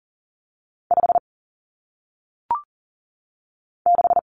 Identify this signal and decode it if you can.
Morse code, heard as a keyed tone, about 39 wpm, 700 Hz. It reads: 5 6
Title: Numbers - 1-Digit 40wpm